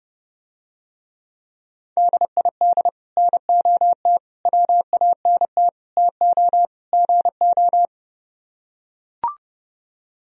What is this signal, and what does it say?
Morse code, heard as a keyed tone, about 31 wpm, 700 Hz. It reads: DID NOT WANT TO GO